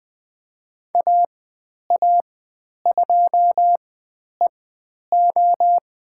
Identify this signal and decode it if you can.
Morse code, heard as a keyed tone, about 20 wpm, spaced as for 10 wpm, 700 Hz. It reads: AA2EO
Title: 102 Lesson 17g - Letter u - Call Signs 20wpm